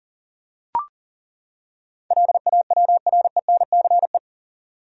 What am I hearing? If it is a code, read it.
Morse code, heard as a keyed tone, about 40 wpm, 700 Hz. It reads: LAWRENCE
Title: USA Common Names - 40wpm